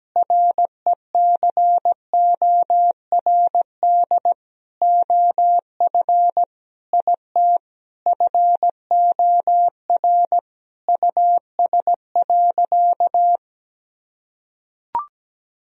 Morse code, heard as a keyed tone, about 17 wpm, 700 Hz. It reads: RECORD OF IT FOR US.
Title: Sentences from Top 1000 Words - Review Part 3 - 17wpm